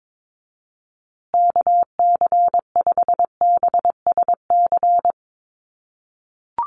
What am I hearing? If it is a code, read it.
Morse code, heard as a keyed tone, about 22 wpm, 700 Hz. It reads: KC5BSC
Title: USA Call Signs Repeated 3x 22wpm